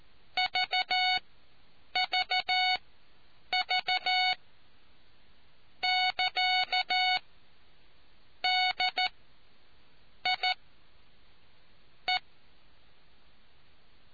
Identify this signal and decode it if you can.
Morse code, heard as a keyed tone, about 14 wpm, spaced as for 6 wpm, 740 Hz. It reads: VVV<KA>DIE